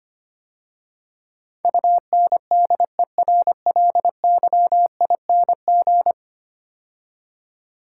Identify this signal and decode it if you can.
Morse code, heard as a keyed tone, about 25 wpm, 700 Hz. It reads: UNDERLYING
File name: 10 Letter Words - First 3 Letters Given - 25wpm